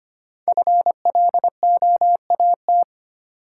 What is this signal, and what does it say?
Morse code, heard as a keyed tone, about 25 wpm, 700 Hz. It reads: FLOAT
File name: 5 Letter Words - First 3 Letters Given - 25wpm